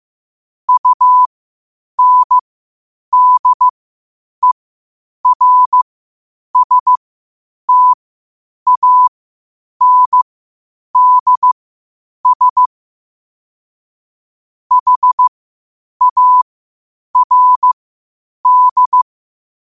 Morse code, heard as a keyed tone, about 15 wpm, spaced as for 8 wpm, 1000 Hz. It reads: UNDERSTANDS HARD